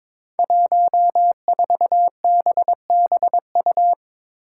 Morse code, heard as a keyed tone, about 23 wpm, 700 Hz. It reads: 14BBU
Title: Five Challenging Letter-Number Combinations 22wpm